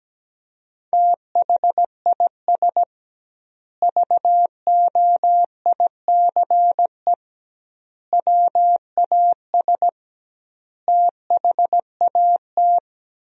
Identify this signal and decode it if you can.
Morse code, heard as a keyed tone, about 17 wpm, 700 Hz. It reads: THIS VOICE WAS THAT
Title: 2x Word-Spacing - Sentences from Top 500 Words - 17wpm